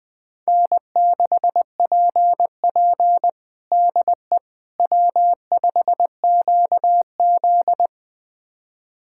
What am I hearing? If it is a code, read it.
Morse code, heard as a keyed tone, about 20 wpm, 700 Hz. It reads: N6PP DE W5QZ
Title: USA Callsign DE Callsign - 4 Characters - 20wpm